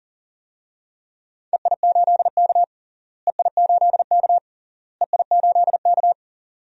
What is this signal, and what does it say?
Morse code, heard as a keyed tone, about 40 wpm, 700 Hz. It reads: EI8K EI8K EI8K
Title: DXCC-100 Call Signs - 4 Characters - Repeated 3x - 40wpm